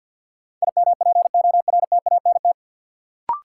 Morse code, heard as a keyed tone, about 52 wpm, 700 Hz. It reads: IMPORTANT
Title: Sets of 3 Words - New Word Drill - 50wpm